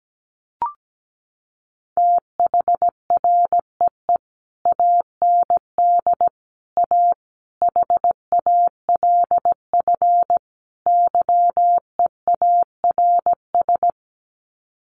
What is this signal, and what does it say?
Morse code, heard as a keyed tone, about 17 wpm, 700 Hz. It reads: THREE AND A HALF YEARS
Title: Sets of 5 Words Encore2 17wpm